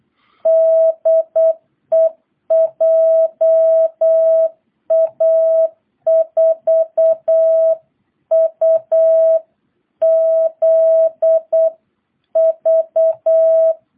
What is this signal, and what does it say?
Morse code, heard as a keyed tone, about 8 wpm, 660 Hz. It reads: DE JA4UZV